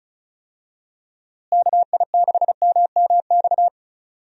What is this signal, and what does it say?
Morse code, heard as a keyed tone, about 36 wpm, 700 Hz. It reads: KI6MMX
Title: USA Call Signs - 6 Characters - Repeated 3x - 35wpm